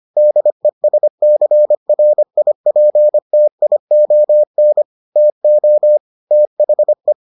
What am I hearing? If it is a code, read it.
Morse code, heard as a keyed tone, about 25 wpm, 600 Hz. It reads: DESCRIPTION TO THE